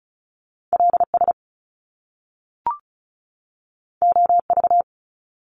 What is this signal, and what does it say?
Morse code, heard as a keyed tone, about 36 wpm, 700 Hz. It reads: LS OV
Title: Top Two Letter Combinations in Words 35wpm